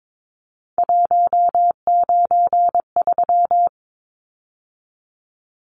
Morse code, heard as a keyed tone, about 22 wpm, 700 Hz. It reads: 193
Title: Numbers - 3-Digit 22wpm